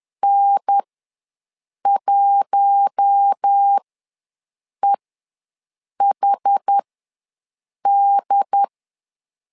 Morse code, heard as a keyed tone, about 10 wpm, spaced as for 6 wpm, 800 Hz. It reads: N1EHD
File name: LEZIONI / 4 Q1S ws 10wpm 2208